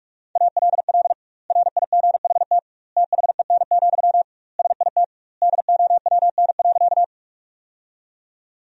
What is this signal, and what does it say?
Morse code, heard as a keyed tone, about 46 wpm, 700 Hz. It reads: ALL RIGHT THEN, SIT DOWN.
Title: 2x Word-Spacing - Sentences from Top 500 Words - 45wpm